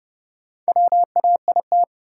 Morse code, heard as a keyed tone, about 30 wpm, 700 Hz. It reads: WAIT
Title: Sentences from Top 500 Words - New Word Drill - 30wpm